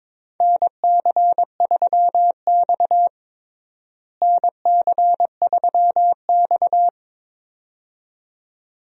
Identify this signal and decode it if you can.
Morse code, heard as a keyed tone, about 22 wpm, 700 Hz. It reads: NC3X NC3X
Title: USA Call Signs - 4 Characters - Repeated 2x - 22wpm